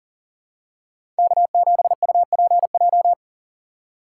Morse code, heard as a keyed tone, about 40 wpm, 700 Hz. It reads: K7UPJ
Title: K1USN SST QSO Practice - 40wpm